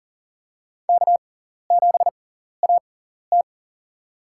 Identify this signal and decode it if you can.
Morse code, heard as a keyed tone, about 43 wpm, spaced as for 14 wpm, 700 Hz. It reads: K7AT
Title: USA Call Signs - 4 Characters - 6x Character Spacing - 40wpm